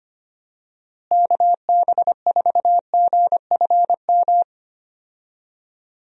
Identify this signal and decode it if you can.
Morse code, heard as a keyed tone, about 25 wpm, 700 Hz. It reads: KB4GFM
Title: USA Call Signs - 6 Characters - Repeated 2x - 25wpm